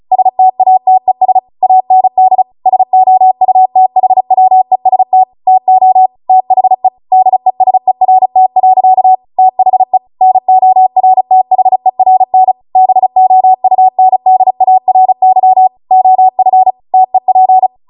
Morse code, heard as a keyed tone, about 36 wpm, 750 Hz. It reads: STATES AND SOUTHWEST TO THE DESERT. THE NORTHERN BOUNDARY OF TEP